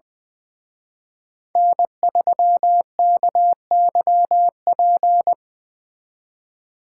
Morse code, heard as a keyed tone, about 23 wpm, 700 Hz. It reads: N3KYP
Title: Speed Racing - USA 1x3 Call Signs 20wpm